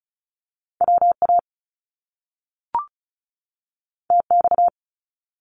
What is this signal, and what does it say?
Morse code, heard as a keyed tone, about 36 wpm, 700 Hz. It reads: WA TX
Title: US States 35wpm